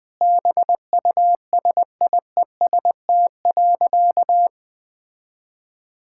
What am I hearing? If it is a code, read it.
Morse code, heard as a keyed tone, about 20 wpm, 700 Hz. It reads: BUSIEST.
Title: Sentences with 7-Word Phrases - Spoken Beginning - 20wpm